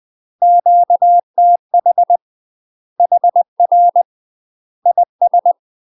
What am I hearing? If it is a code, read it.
Morse code, heard as a keyed tone, about 20 wpm, 700 Hz. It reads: QTH HR IS
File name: UK MASTER QSOs TRAINING / 20WPM / UK MASTER QSOs 0024